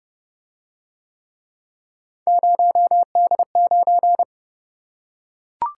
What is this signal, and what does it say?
Morse code, heard as a keyed tone, about 29 wpm, 700 Hz. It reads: 0D9